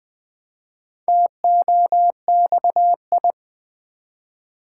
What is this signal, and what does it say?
Morse code, heard as a keyed tone, about 21 wpm, 700 Hz. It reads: TOXI
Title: Build A Word - 1 Letter At A Time - 5-Letter Words - 20wpm